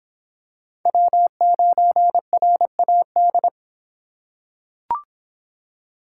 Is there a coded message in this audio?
Yes: Morse code from 0.9 to 3.5 s, heard as a keyed tone, about 22 wpm, 700 Hz. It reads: W9RAD